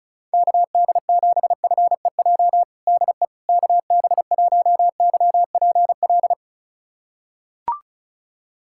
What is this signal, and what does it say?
Morse code, heard as a keyed tone, about 34 wpm, 700 Hz. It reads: KD7FEJ DE KB1YPL
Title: USA Callsign DE Callsign - 6 Characters - 35wpm